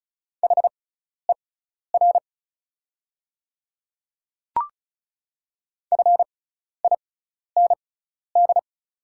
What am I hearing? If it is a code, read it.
Morse code, heard as a keyed tone, about 36 wpm, spaced as for 12 wpm, 690 Hz. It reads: HER FIND
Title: 6x Character Spacing - Top 500 Words - 35wpm